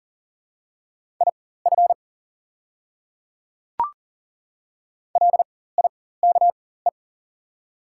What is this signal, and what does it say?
Morse code, heard as a keyed tone, about 40 wpm, spaced as for 19 wpm, 690 Hz. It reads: IF LIKE
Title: 4x Character Spacing - Top 100 Words - 40wpm